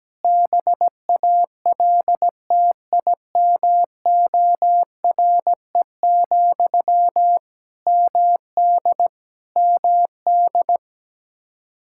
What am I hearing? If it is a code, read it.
Morse code, heard as a keyed tone, about 17 wpm, 700 Hz. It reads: BALTIMORE, MD MD
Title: K1USN SST QSO Practice 17wpm